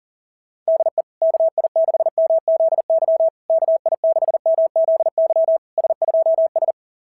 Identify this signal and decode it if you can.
Morse code, heard as a keyed tone, about 43 wpm, 690 Hz. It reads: DE KI6MZY KI6MZY S2S
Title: SOTA QSO Practice 40wpm